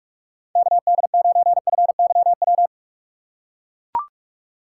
Morse code, heard as a keyed tone, about 42 wpm, 700 Hz. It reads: KD9FYW
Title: USA Call Signs - 6 Characters - 2x Character Spacing - 45wpm